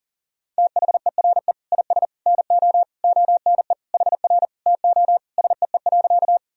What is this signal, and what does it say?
Morse code, heard as a keyed tone, about 37 wpm, 700 Hz. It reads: THERE IS NO ONE HR TO SEE.
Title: Sentences from Top 200 Words and Abbreviations 40wpm